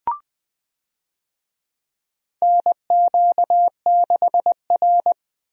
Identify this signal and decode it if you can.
Morse code, heard as a keyed tone, about 20 wpm, 700 Hz. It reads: NQ6R
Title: USA Call Signs - 4 Characters - Repeated 2x - 20wpm